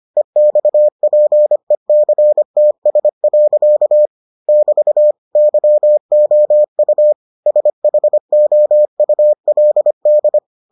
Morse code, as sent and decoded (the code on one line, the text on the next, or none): . -..- .--. -.-. - ... .-.-.- / -...- / -.-- --- ..- / ... .... --- ..- .-.. -..
EXPECTS. <BT> YOU SHOULD